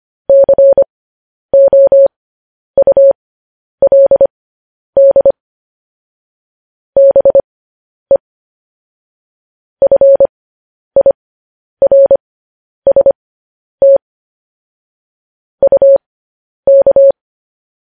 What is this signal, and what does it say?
Morse code, heard as a keyed tone, about 25 wpm, spaced as for 10 wpm, 550 Hz. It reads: COULD BE FIRST UK
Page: news-headlines-in-morse-code-at-10-wpm